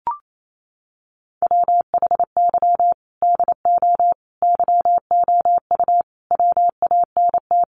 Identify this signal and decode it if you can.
Morse code, heard as a keyed tone, about 28 wpm, 700 Hz. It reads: WHY DO YOU WANT
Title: Sentences from Top 200 Words 28wpm